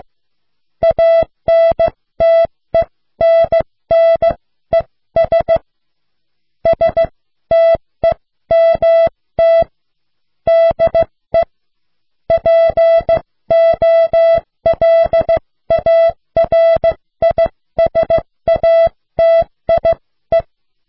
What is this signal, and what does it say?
Morse code, heard as a keyed tone, about 16 wpm, spaced as for 13 wpm, 660 Hz. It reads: ANTENNES STEMT DE POLARISATIE